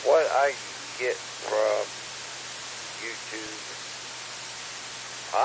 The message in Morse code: none

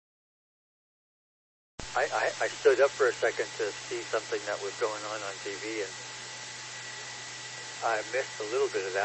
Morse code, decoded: none

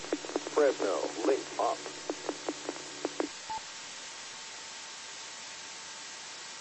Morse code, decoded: none

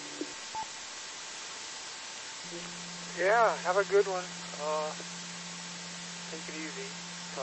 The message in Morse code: none

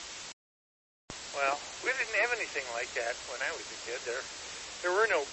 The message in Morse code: none

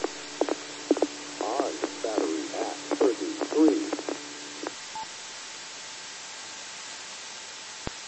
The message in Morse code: none

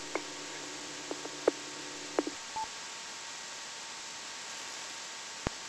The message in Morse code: none

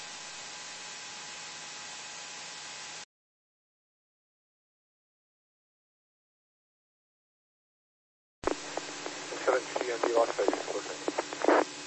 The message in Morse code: none

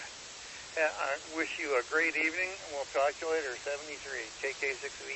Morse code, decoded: none

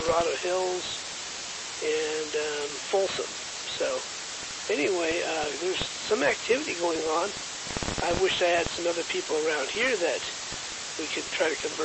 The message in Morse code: none

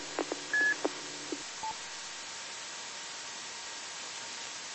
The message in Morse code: none

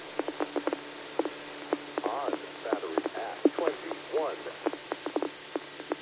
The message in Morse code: none

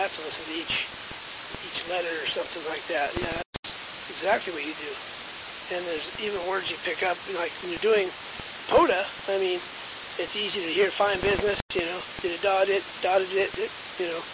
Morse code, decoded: none

